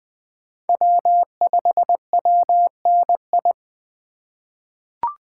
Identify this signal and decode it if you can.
Morse code, heard as a keyed tone, about 20 wpm, 700 Hz. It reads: W5WNI
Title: USA Call Signs - 5 Characters - 4x Character Spacing - 20wpm